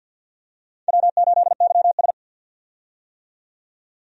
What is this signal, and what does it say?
Morse code, heard as a keyed tone, about 52 wpm, 700 Hz. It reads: W8YS
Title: USA Call Signs - 4 Characters - 50wpm